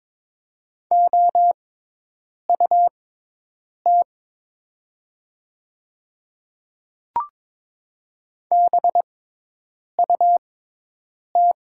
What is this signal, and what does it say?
Morse code, heard as a keyed tone, about 22 wpm, spaced as for 8 wpm, 690 Hz. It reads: OUT BUT
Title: 6x Character Spacing - Top 100 Words - 22wpm